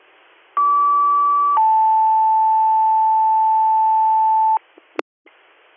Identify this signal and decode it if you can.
Two-tone page, heard as a one-second tone, then a three-second tone, about 1150 then 880 Hz. It is heard from 0.6 to 4.6 s.